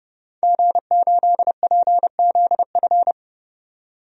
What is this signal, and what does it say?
Morse code, heard as a keyed tone, about 31 wpm, 700 Hz. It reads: G8PZF
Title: DXCC-100 Call Signs - 5 Characters - 30wpm